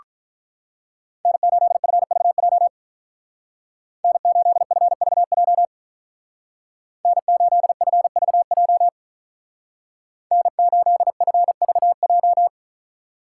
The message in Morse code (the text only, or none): N8FVJ N8FVJ N8FVJ N8FVJ